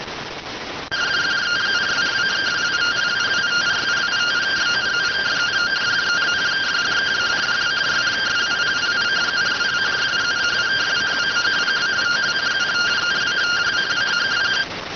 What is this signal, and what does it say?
Two-tone teleprinter signal, about 170 Hz shift.